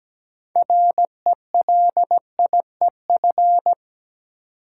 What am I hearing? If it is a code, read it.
Morse code, heard as a keyed tone, about 17 wpm, 700 Hz. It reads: RELIEF
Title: Six Letter Words - 17wpm